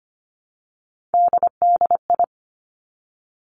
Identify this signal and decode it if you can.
Morse code, heard as a keyed tone, about 25 wpm, 700 Hz. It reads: DDI
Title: Three Challenging Letter-Number Combinations 25wpm